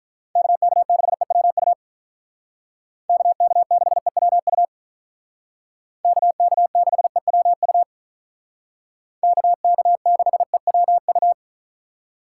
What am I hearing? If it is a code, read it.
Morse code, heard as a keyed tone, about 45 wpm, 700 Hz. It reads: KK6EWU KK6EWU KK6EWU KK6EWU